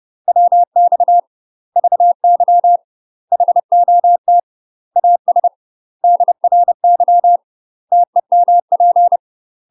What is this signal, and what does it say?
Morse code, heard as a keyed tone, about 30 wpm, 700 Hz. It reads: WX VY HOT AS DRY TEMP
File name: UK MASTER QSOs TRAINING / 30WPM / UK MASTER QSOs 0042